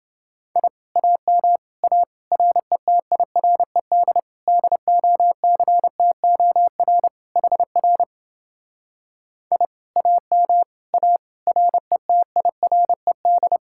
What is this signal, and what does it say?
Morse code, heard as a keyed tone, about 29 wpm, 700 Hz. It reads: I AM A RETIRED DOCTOR HR I AM A RETIRED